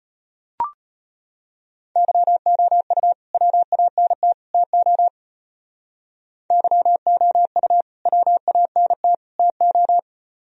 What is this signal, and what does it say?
Morse code, heard as a keyed tone, about 32 wpm, 700 Hz. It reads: YOU WANT TO YOU WANT TO